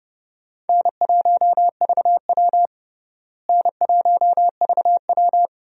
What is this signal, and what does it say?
Morse code, heard as a keyed tone, about 29 wpm, 700 Hz. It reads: N1VW N1VW
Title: USA Call Signs - 4 Characters - Repeated 2x - 30wpm